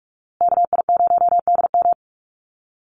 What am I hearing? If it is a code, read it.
Morse code, heard as a keyed tone, about 44 wpm, 690 Hz. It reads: KI0DM
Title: USA Call Signs Repeated 3x 45wpm